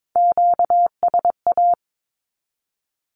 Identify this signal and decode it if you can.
Morse code, heard as a keyed tone, about 22 wpm, 700 Hz. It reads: QSA
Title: Q-Codes 22wpm